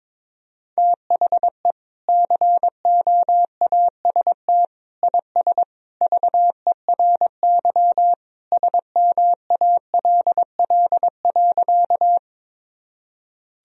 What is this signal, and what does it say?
Morse code, heard as a keyed tone, about 22 wpm, 700 Hz. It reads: THE COAST IS VERY SMALL.
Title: Sentences from Top 1000 Words - Review Part 4 - 22wpm